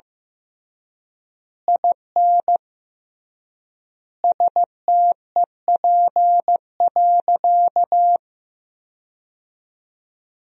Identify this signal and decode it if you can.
Morse code, heard as a keyed tone, about 15 wpm, 700 Hz. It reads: IN STEP.
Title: 3x Word-Spacing - Sentences from Top 400 Words - 15wpm